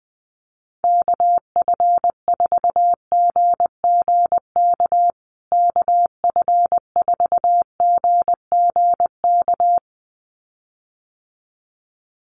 Morse code, heard as a keyed tone, about 20 wpm, 700 Hz. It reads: KF4GGK KF4GGK
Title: USA Call Signs Repeated 2x 20wpm